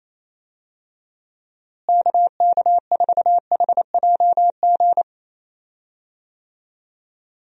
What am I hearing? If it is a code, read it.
Morse code, heard as a keyed tone, about 28 wpm, 700 Hz. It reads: KK4HJG
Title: USA Call Signs - 6 Characters - Repeated 3x - 28wpm